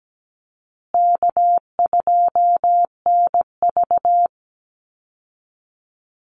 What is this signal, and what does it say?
Morse code, heard as a keyed tone, about 17 wpm, 700 Hz. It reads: K2NV
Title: Easy Call Signs Repeated 2x - 17wpm